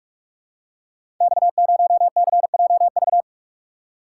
Morse code, heard as a keyed tone, about 44 wpm, 700 Hz. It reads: X0CJV
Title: Code Groups - 5 Letter:Numbers - 45wpm